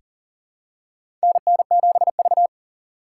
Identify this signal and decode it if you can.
Morse code, heard as a keyed tone, about 37 wpm, 700 Hz. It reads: NN7V
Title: USA Call Signs - 4 Characters - 4x Character Spacing - 40wpm